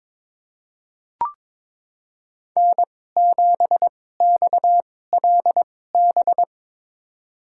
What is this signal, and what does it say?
Morse code, heard as a keyed tone, about 22 wpm, spaced as for 16 wpm, 700 Hz. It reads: N7XLB
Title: USA Call Signs - 5 Characters - 2x Character Spacing - 22wpm